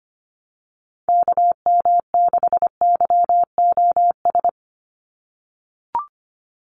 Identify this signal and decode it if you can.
Morse code, heard as a keyed tone, about 25 wpm, 700 Hz. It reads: KM6YOS
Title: USA Call Signs 25wpm